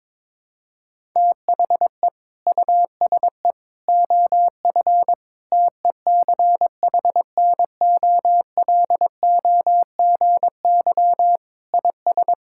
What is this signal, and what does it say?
Morse code, heard as a keyed tone, about 22 wpm, 700 Hz. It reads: THE USE OF TECHNOLOGY IS
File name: practice / 22wpm / Sentences with 2-Word Phrases - 22wpm